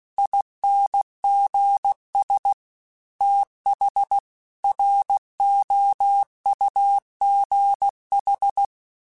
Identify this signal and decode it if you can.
Morse code, heard as a keyed tone, about 16 wpm, 800 Hz. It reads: INGS TH ROUGH